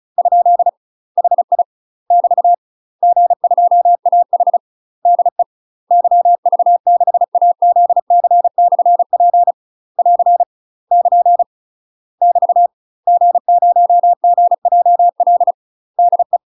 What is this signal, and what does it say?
Morse code, heard as a keyed tone, about 34 wpm, 700 Hz. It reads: ? HI <BT> G2AH DE YV6AZC/P <AR> <KN> <BT> G0GJL DE
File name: UK MASTER QSOs TRAINING/35WPM